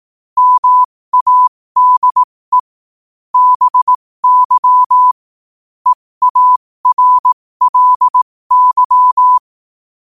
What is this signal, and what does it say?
Morse code, heard as a keyed tone, about 18 wpm, spaced as for 15 wpm, 1000 Hz. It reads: MADE BY EARLY
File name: qotdfiles / 15wpm